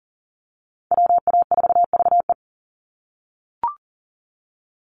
Morse code, heard as a keyed tone, about 40 wpm, 690 Hz. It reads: WA4VE
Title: USA Call Signs 40wpm